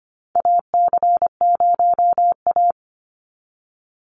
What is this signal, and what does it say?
Morse code, heard as a keyed tone, about 25 wpm, 700 Hz. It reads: AC0A
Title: NA SOTA Chaser Callsigns 25wpm